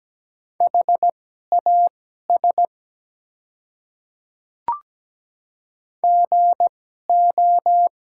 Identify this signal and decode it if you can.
Morse code, heard as a keyed tone, about 17 wpm, spaced as for 12 wpm, 700 Hz. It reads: HAS GO
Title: 2x Character Spacing - Top 300 Words - 17wpm